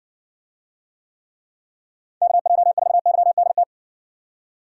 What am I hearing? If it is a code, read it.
Morse code, heard as a keyed tone, about 60 wpm, 700 Hz. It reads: KJ4YDT